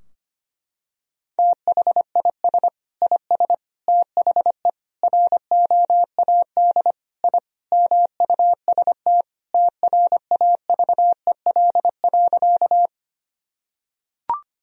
Morse code, heard as a keyed tone, about 25 wpm, 700 Hz. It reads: THIS IS THE ROAD I MUST TRAVEL.
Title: Sentences from Top 1000 Words - Review Part 2 - 25wpm